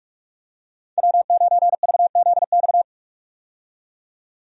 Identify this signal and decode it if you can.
Morse code, heard as a keyed tone, about 47 wpm, 700 Hz. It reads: W9VZX
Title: USA Call Signs - 5 Characters - 45wpm